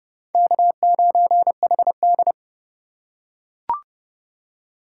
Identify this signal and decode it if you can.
Morse code, heard as a keyed tone, about 25 wpm, 690 Hz. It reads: K9HD
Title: Speed Racing - USA 1x2 Call Signs 20wpm